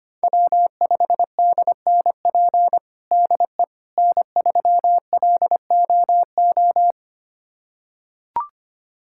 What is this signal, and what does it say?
Morse code, heard as a keyed tone, about 25 wpm, 700 Hz. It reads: W5DNP DE N3LOO
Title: USA Callsign DE Callsign - 5 Characters - 25wpm